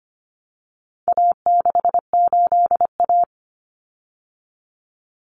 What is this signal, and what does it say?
Morse code, heard as a keyed tone, about 25 wpm, 700 Hz. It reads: A68A
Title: Common Cut Numbers - 4-Digit 25wpm